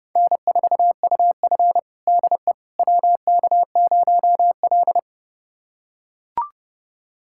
Morse code, heard as a keyed tone, about 30 wpm, 700 Hz. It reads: N4UF DE WK0L
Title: USA Callsign DE Callsign - 4 Characters - 30wpm